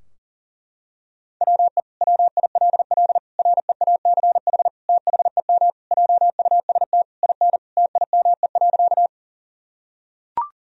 Morse code, heard as a keyed tone, about 40 wpm, 700 Hz. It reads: WE WILL REACH THEM JUST IN TIME.
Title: Sentences from Top 1000 Words - Review Part 2 - 40wpm